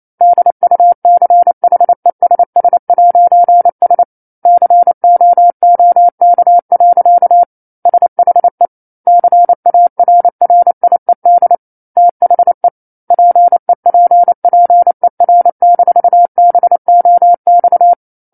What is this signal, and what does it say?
Morse code, heard as a keyed tone, about 28 wpm, 700 Hz. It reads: DUCHESS'S COOK. SHE CARRIED THE PEPPER-BOX
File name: 28 WPM